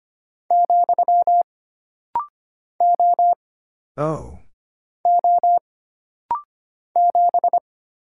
Morse code, heard as a keyed tone, about 25 wpm, 700 Hz. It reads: , O O 7
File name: Single Character (Letter:Number:Punctuation:Prosign) - Rapid-Fire - 25wpm